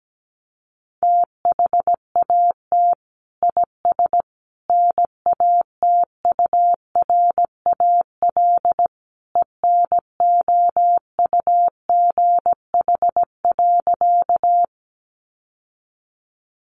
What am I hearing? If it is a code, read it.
Morse code, heard as a keyed tone, about 17 wpm, 700 Hz. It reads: THAT IS NATURAL ENOUGH.
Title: Sentences from Top 700 Words 17wpm